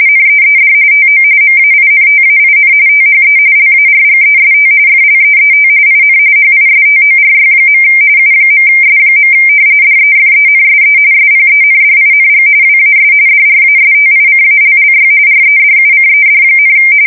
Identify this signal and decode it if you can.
RTTY, about 45.45 baud. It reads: NEED TO MAINTAIN THE LARGEST FLEET IN HISTORY. U.S. DEFENSE SPENDING WAS DRASTICALLY REDUCED,
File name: fsk170 / battleshipnj